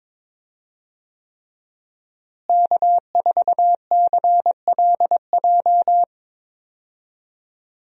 Morse code, heard as a keyed tone, about 22 wpm, 700 Hz. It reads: K4CLJ
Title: USA Call Signs - 5 Characters - Repeated 3x - 22wpm